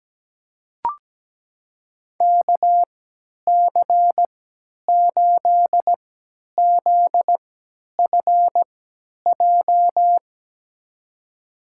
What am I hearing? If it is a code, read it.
Morse code, heard as a keyed tone, about 17 wpm, spaced as for 10 wpm, 700 Hz. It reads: KC8ZFJ
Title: USA Call Signs - 6 Characters - 3x Character Spacing - 17wpm